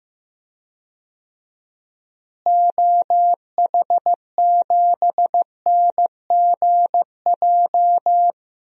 Morse code, heard as a keyed tone, about 15 wpm, 700 Hz. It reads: OH7NGJ
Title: DXCC-100 Call Signs - 6 Characters - Repeated 3x - 15wpm